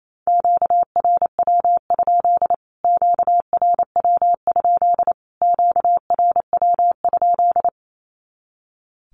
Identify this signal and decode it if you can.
Morse code, heard as a keyed tone, about 28 wpm, 700 Hz. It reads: QRW? QRW? QRW?
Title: Q-Codes Repeated 3x 28wpm